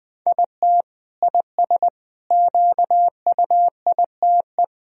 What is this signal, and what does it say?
Morse code, heard as a keyed tone, about 20 wpm, 700 Hz. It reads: IT IS QUITE
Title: Sentences from Top 1000 Words - Review Part 3 - 20wpm